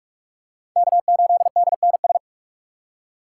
Morse code, heard as a keyed tone, about 38 wpm, 700 Hz. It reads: K8DNS